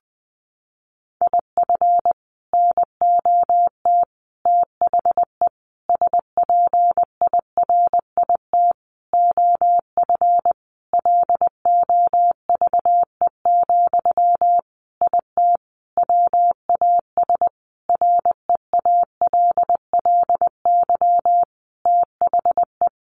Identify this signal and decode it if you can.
Morse code, heard as a keyed tone, about 20 wpm, 700 Hz. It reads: IF NOT THE SPIRIT OF LOVE, IT WAS REALLY THE
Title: Sentences from Top 1400 Words 20wpm